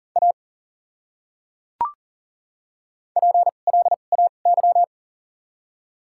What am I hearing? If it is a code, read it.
Morse code, heard as a keyed tone, about 40 wpm, spaced as for 29 wpm, 700 Hz. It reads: A PLAY
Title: 2x Character Spacing - Top 200 Words - 40wpm